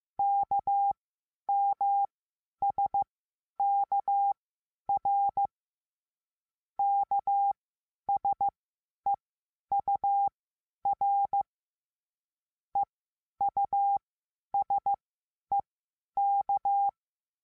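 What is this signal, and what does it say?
Morse code, heard as a keyed tone, about 15 wpm, spaced as for 10 wpm, 800 Hz. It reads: KMSKR KSEUR EUSEK
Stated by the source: Morse code lesson adding the letter S.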